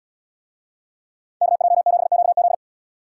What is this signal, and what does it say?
Morse code, heard as a keyed tone, about 70 wpm, 690 Hz. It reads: KJ7QZ